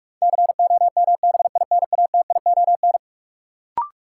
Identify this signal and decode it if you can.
Morse code, heard as a keyed tone, about 43 wpm, 700 Hz. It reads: COMBINATION
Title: Sentences from Top 1300 Words - New Word Drill Alt - 45wpm